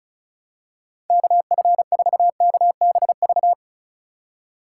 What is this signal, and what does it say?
Morse code, heard as a keyed tone, about 36 wpm, 700 Hz. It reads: KF4KBV
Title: USA Call Signs - 6 Characters - 3x Character Spacing - 35wpm